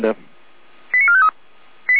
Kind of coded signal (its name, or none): five-tone selective call